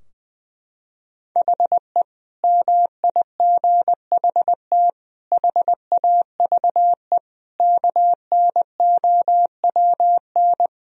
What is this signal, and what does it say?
Morse code, heard as a keyed tone, about 19 wpm, 700 Hz. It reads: HE MIGHT HAVE KNOWN
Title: Sentences from Top 1000 Words - Review Part 3 - 20wpm